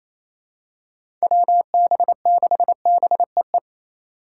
Morse code, heard as a keyed tone, about 28 wpm, 700 Hz. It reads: WB6BEE
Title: K1USN SST QSO Practice - 28wpm